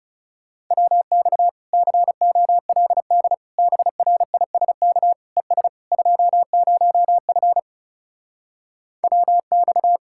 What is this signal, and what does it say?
Morse code, heard as a keyed tone, about 40 wpm, 700 Hz. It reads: WX COLD BRISK ES 20F WX